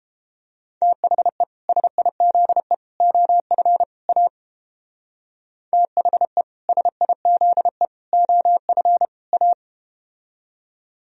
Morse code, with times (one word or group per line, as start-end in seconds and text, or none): 0.8-1.4 THE
1.7-2.7 SIZE
3.0-3.8 OF
4.1-4.3 A
5.7-6.4 THE
6.7-7.9 SIZE
8.1-9.1 OF
9.3-9.5 A